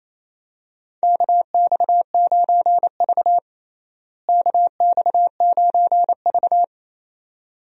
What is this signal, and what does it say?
Morse code, heard as a keyed tone, about 28 wpm, 700 Hz. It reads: KX9V KX9V
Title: USA Call Signs - 4 Characters - Repeated 2x - 28wpm